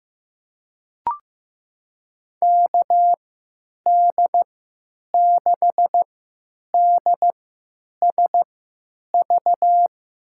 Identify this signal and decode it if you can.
Morse code, heard as a keyed tone, about 15 wpm, spaced as for 9 wpm, 700 Hz. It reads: KD6DSV
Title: USA Call Signs - 6 Characters - 3x Character Spacing - 15wpm